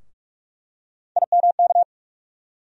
Morse code, heard as a keyed tone, about 42 wpm, 700 Hz. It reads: IMK
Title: USA Call Sign Postfixes - 3 Letters 45wpm